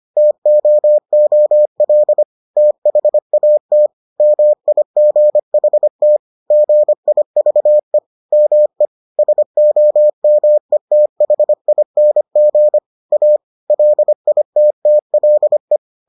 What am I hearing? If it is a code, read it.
Morse code, heard as a keyed tone, about 25 wpm, 600 Hz. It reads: TOOL THAT MIGHT GIVE ME SOMETHING A LITTLE